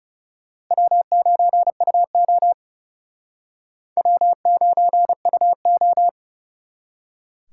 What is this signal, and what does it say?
Morse code, heard as a keyed tone, about 35 wpm, 700 Hz. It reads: W9UO W9UO
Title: Speed Racing - USA Call Signs 30wpm